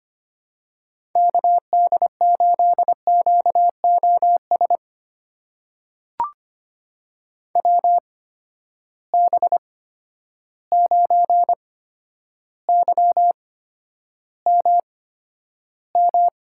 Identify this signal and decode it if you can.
Morse code, heard as a keyed tone, about 25 wpm, spaced as for 11 wpm, 700 Hz. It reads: KD8QOS WB9YMM
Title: USA Call Signs - 6 Characters - 8x Character Spacing - 25wpm